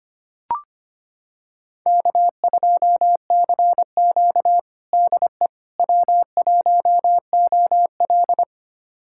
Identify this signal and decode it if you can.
Morse code, heard as a keyed tone, about 25 wpm, 700 Hz. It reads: K2CQ DE W1OL
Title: USA Callsign DE Callsign - 4 Characters - 25wpm